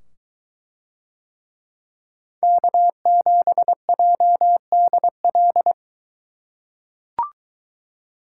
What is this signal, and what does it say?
Morse code, heard as a keyed tone, about 23 wpm, 700 Hz. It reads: K7JDL